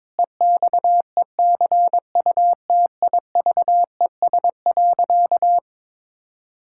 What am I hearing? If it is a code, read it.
Morse code, heard as a keyed tone, about 22 wpm, 700 Hz. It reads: EXECUTIVES.
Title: Pangram Challenge v2 - 22wpm